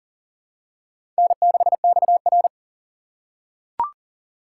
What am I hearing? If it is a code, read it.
Morse code, heard as a keyed tone, about 40 wpm, 700 Hz. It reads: N6XR
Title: USA Call Signs - 4 Characters - 6x Character Spacing - 40wpm